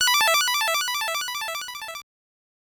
There is no code.